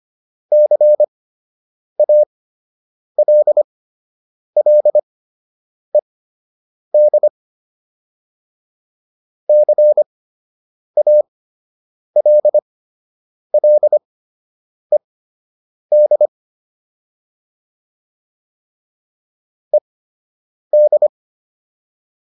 Morse code, heard as a keyed tone, about 25 wpm, spaced as for 8 wpm, 600 Hz. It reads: CALLED CALLED ED